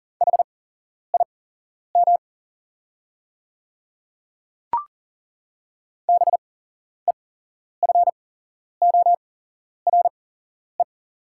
Morse code, heard as a keyed tone, about 40 wpm, spaced as for 11 wpm, 690 Hz. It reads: HIM BEFORE
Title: 8x Character Spacing - Top 500 Words - 40wpm